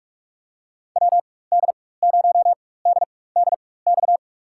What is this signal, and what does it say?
Morse code, heard as a keyed tone, about 46 wpm, spaced as for 21 wpm, 700 Hz. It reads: WD0DDX
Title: USA Call Signs - 6 Characters - 4x Character Spacing - 45wpm